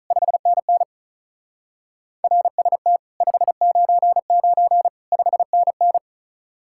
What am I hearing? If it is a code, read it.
Morse code, heard as a keyed tone, about 45 wpm, 700 Hz. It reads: 5NN RST 599 5NN